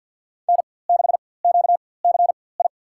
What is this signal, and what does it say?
Morse code, heard as a keyed tone, about 48 wpm, spaced as for 23 wpm, 690 Hz. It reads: N6QCI
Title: USA Call Signs - 5 Characters - 4x Character Spacing - 50wpm